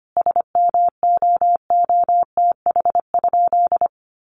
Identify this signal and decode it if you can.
Morse code, heard as a keyed tone, about 25 wpm, 700 Hz. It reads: SMOOTH?
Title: Sentences from Top 1400 Words 25wpm